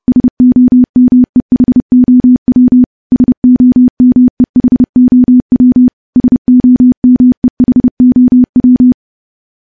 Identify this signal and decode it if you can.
Morse code, heard as a keyed tone, about 28 wpm, 250 Hz. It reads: SOMEHOW SOMEHOW SOMEHOW
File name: morse / morse-training-30WPM